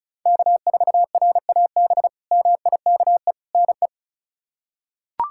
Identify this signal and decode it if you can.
Morse code, heard as a keyed tone, about 34 wpm, 700 Hz. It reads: K4RAB MIKE NE
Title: K1USN SST QSO Practice - 35wpm